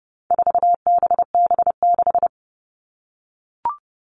Morse code, heard as a keyed tone, about 30 wpm, 700 Hz. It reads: 4BB6
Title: All Cut Numbers - 4-Digit 30wpm